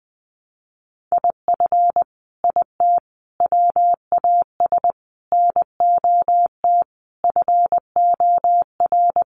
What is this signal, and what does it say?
Morse code, heard as a keyed tone, about 20 wpm, 700 Hz. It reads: IF IT WAS NOT FOR
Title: Sets of 5 Words Encore 20wpm